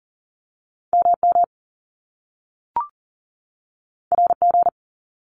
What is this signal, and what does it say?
Morse code, heard as a keyed tone, about 40 wpm, 700 Hz. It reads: MM RG
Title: Top Two Letter Combinations in Words 40wpm